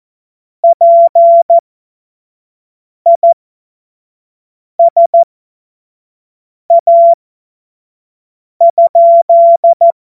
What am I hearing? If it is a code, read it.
Morse code, heard as a keyed tone, about 14 wpm, spaced as for 5 wpm, 680 Hz. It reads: PISA?
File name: Morsekurs / Witze / Witz 07.November 2023 25 F70